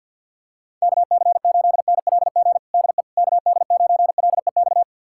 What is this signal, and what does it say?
Morse code, heard as a keyed tone, about 52 wpm, 700 Hz. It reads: KC8NLG DE KD9LEX
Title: USA Callsign DE Callsign - 6 Characters - 50wpm